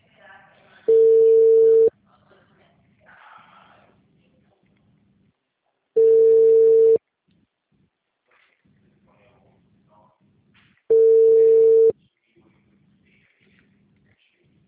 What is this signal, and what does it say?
Ringback tone (China, 450 Hz), with speech.
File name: out- -2014- - - .12815